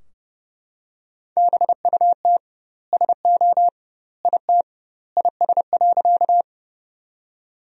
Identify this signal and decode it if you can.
Morse code, heard as a keyed tone, about 29 wpm, 700 Hz. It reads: BUT SO IT IS.
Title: 2x Word-Spacing - Sentences from Top 100 Words - 30wpm